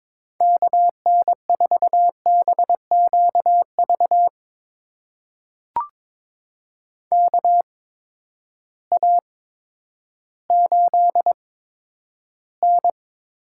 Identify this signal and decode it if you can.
Morse code, heard as a keyed tone, about 22 wpm, spaced as for 11 wpm, 700 Hz. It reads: KN4BQV KA8N
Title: USA Call Signs - 6 Characters - 8x Character Spacing - 22wpm